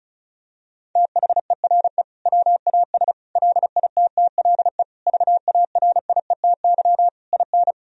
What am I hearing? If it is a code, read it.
Morse code, heard as a keyed tone, about 34 wpm, 700 Hz. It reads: THERE WAS LITTLE VARIETY IN
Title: Sentences from Top 1000 Words - Review Part 4 - 35wpm